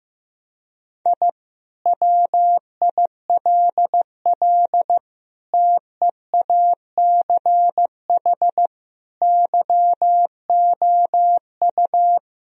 Morse code, heard as a keyed tone, about 15 wpm, 700 Hz. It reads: I WILL TEACH YOU
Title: Sentences from Top 1000 Words - Review Part 3 - 15wpm